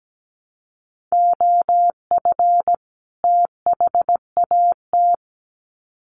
Morse code, heard as a keyed tone, about 17 wpm, 700 Hz. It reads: OF THAT
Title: Sets of 2 Words 17wpm